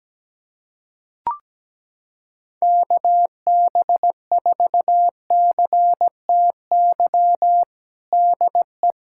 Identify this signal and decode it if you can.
Morse code, heard as a keyed tone, about 17 wpm, 700 Hz. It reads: KB4CTY DE
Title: USA Callsign DE Callsign 17wpm